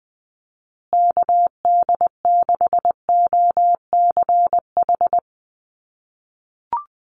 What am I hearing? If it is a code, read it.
Morse code, heard as a keyed tone, about 20 wpm, 700 Hz. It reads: KD6OCH
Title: USA Call Signs Repeated 3x 20wpm